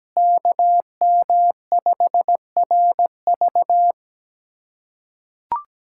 Morse code, heard as a keyed tone, about 17 wpm, 700 Hz. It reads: KM5RV
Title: USA Callsign DE Callsign 17wpm